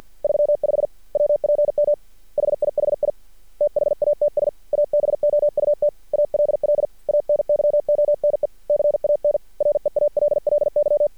Morse code, heard as a keyed tone, about 50 wpm, 590 Hz. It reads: 35 WPM HIHI THATS ABOUT ALL ANYONE CAN REALLY
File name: archived / CRSnet